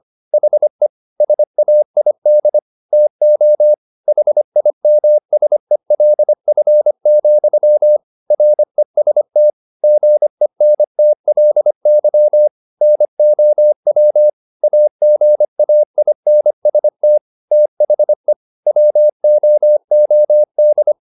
Morse code, heard as a keyed tone, about 25 wpm, 600 Hz. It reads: HE SAID TO HIMSELF, REST GENTLY NOW AGAINST THE WOOD
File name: the old man and the sea part 1 25wpm